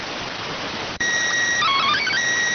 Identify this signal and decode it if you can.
SSTV (slow-scan television), PD 290.